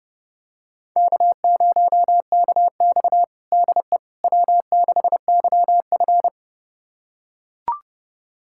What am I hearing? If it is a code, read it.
Morse code, heard as a keyed tone, about 30 wpm, 700 Hz. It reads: K0KX DE W6YF
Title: USA Callsign DE Callsign - 4 Characters - 30wpm